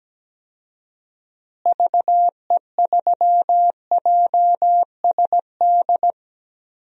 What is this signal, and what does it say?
Morse code, heard as a keyed tone, about 17 wpm, 700 Hz. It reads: VE3JSD
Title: DXCC-100 Call Signs - 6 Characters - Repeated 3x - 17wpm